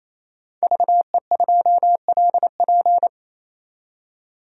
Morse code, heard as a keyed tone, about 28 wpm, 700 Hz. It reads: VE2LP
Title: DXCC-100 Call Signs - 5 Characters - 28wpm